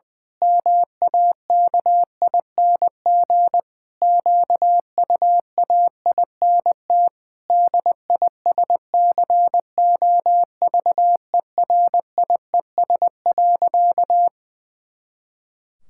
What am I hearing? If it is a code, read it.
Morse code, heard as a keyed tone, about 20 wpm, 700 Hz. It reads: MAKING QUAINT DISCOVERIES.
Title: Pangram Challenge v2 - 20wpm